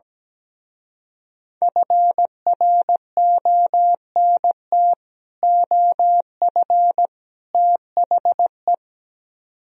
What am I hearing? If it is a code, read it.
Morse code, heard as a keyed tone, about 17 wpm, 700 Hz. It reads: FRONT OF THE
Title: Speed-Racing - Sets of 3 Words 17wpm